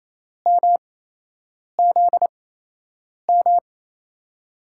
Morse code, heard as a keyed tone, about 28 wpm, spaced as for 8 wpm, 700 Hz. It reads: MZM
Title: USA Call Signs - 6 Characters - 8x Character Spacing - 28wpm